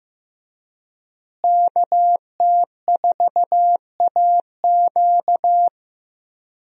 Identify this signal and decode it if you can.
Morse code, heard as a keyed tone, about 15 wpm, 700 Hz. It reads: KT4AQ
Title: USA Call Signs - 5 Characters - Repeated 3x - 15wpm